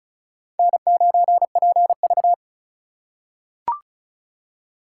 Morse code, heard as a keyed tone, about 36 wpm, 700 Hz. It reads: N9PV
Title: USA Call Signs - 4 Characters - 4x Character Spacing - 35wpm